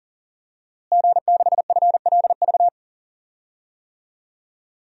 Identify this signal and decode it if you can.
Morse code, heard as a keyed tone, about 37 wpm, 700 Hz. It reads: G6FLV
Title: DXCC-100 Call Signs - 5 Characters - Repeated 3x - 40wpm